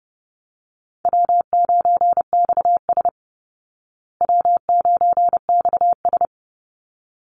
Morse code, heard as a keyed tone, about 30 wpm, 700 Hz. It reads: W9XS W9XS
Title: Call Signs Repeated 3x - 30wpm